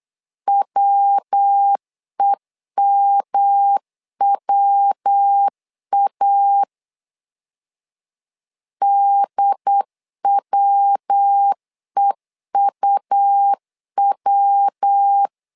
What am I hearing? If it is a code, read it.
Morse code, heard as a keyed tone, about 8 wpm, 800 Hz. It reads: WEMWA DWEUW
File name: LEZIONI / 3 MWRO / 3 MW 8wpm 2501